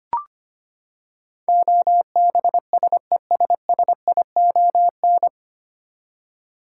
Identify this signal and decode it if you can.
Morse code, heard as a keyed tone, about 25 wpm, 700 Hz. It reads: OBSESSION
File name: Nine Letter Words - 25wpm